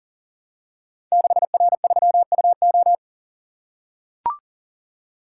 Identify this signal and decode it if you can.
Morse code, heard as a keyed tone, about 42 wpm, 700 Hz. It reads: 6R3UO
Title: Code Groups - 5 Letter:Numbers - 40wpm